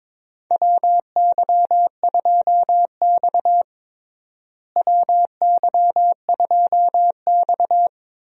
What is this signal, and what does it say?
Morse code, heard as a keyed tone, about 22 wpm, 700 Hz. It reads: WY2X WY2X
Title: USA Call Signs - 4 Characters - Repeated 2x - 22wpm